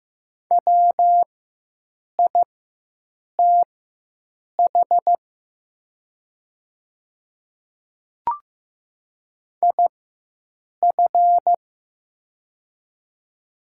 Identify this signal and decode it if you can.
Morse code, heard as a keyed tone, about 15 wpm, spaced as for 7 wpm, 700 Hz. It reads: WITH IF
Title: 4x Character Spacing - Top 300 Words - 15wpm